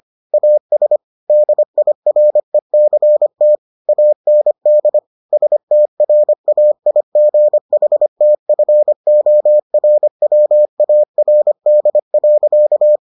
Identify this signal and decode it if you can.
Morse code, heard as a keyed tone, about 25 wpm, 600 Hz. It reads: AS DIRECT AND STRAIGHTFORWARD.